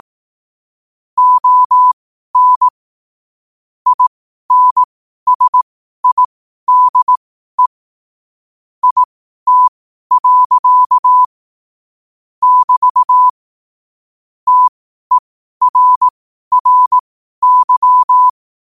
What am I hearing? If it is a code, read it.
Morse code, heard as a keyed tone, about 18 wpm, spaced as for 12 wpm, 1000 Hz. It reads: ON INSIDE IT. <BT> TERRY
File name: qotdfiles / 12wpm